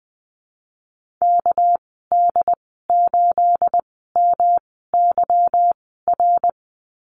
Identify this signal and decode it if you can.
Morse code, heard as a keyed tone, about 20 wpm, spaced as for 14 wpm, 700 Hz. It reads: KD8MYR
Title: 219 Lesson 35g - Number 0 - Call Signs 20wpm